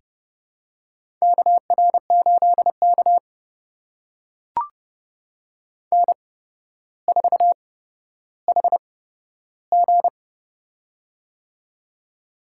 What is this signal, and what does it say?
Morse code, heard as a keyed tone, about 31 wpm, spaced as for 13 wpm, 700 Hz. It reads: KR8K N4HG
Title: USA Call Signs - 4 Characters - 8x Character Spacing - 30wpm